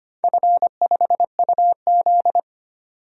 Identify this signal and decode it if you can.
Morse code, heard as a keyed tone, about 25 wpm, 700 Hz. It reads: F5UZ
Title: DXCC-100 Call Signs - 4 Characters - 25wpm